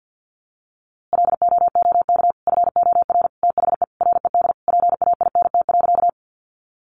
Morse code, heard as a keyed tone, about 50 wpm, 700 Hz. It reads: LOOK FOR THE RED PAINT.
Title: A Sound Of Thunder 50wpm